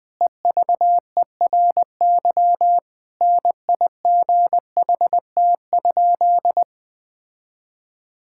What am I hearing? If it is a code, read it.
Morse code, heard as a keyed tone, about 20 wpm, 700 Hz. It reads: EVERY NIGHT?
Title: Sentences from Top 1000 Words - Review Part 5 - 20wpm